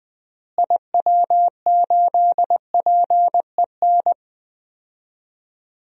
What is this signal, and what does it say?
Morse code, heard as a keyed tone, about 20 wpm, 700 Hz. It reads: IW8PEN
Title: DXCC-100 Call Signs - 6 Characters - Repeated 2x - 20wpm